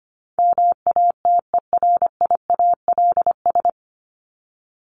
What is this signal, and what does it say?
Morse code, heard as a keyed tone, about 25 wpm, 700 Hz. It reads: MATERIALS
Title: ES:S Suffix - Top 100 Words with Base Words 25wpm